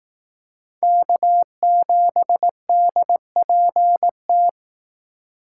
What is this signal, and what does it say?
Morse code, heard as a keyed tone, about 18 wpm, 700 Hz. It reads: K7DPT
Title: Speed Racing - USA 1x3 Call Signs 15wpm